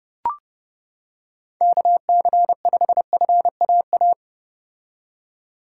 Morse code, heard as a keyed tone, about 31 wpm, 700 Hz. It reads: KC5FAA